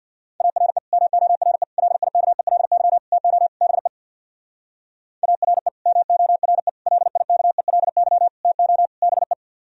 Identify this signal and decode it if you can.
Morse code, heard as a keyed tone, about 58 wpm, 700 Hz. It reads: ARE MORE LIKELY TO BE ARE MORE LIKELY TO BE
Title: Speed-Racing - Sets of 5 Words 50wpm